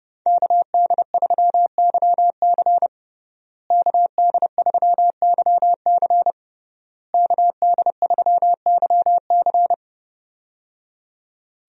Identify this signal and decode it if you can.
Morse code, heard as a keyed tone, about 30 wpm, 700 Hz. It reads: KD3YC KD3YC KD3YC
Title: USA Call Signs - 5 Characters - Repeated 3x - 30wpm